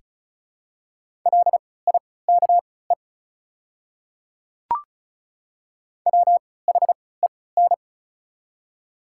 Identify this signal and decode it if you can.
Morse code, heard as a keyed tone, about 36 wpm, spaced as for 20 wpm, 690 Hz. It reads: LIKE WHEN
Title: 3x Character Spacing - Top 100 Words - 35wpm